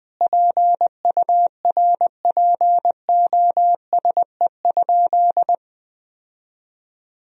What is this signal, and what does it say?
Morse code, heard as a keyed tone, about 21 wpm, 700 Hz. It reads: PURPOSE?
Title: Sentences from Top 1000 Words - Review Part 2 - 20wpm